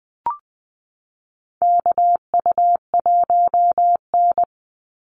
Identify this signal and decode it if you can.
Morse code, heard as a keyed tone, about 20 wpm, 700 Hz. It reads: KU1N
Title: Call Signs Repeated 3x - 20wpm